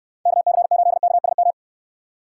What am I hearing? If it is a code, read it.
Morse code, heard as a keyed tone, about 69 wpm, 700 Hz. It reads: KC8KSM